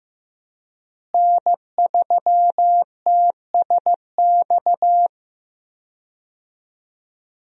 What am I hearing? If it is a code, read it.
Morse code, heard as a keyed tone, about 15 wpm, 700 Hz. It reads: N3TSX
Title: USA Call Signs - 5 Characters - Repeated 2x - 15wpm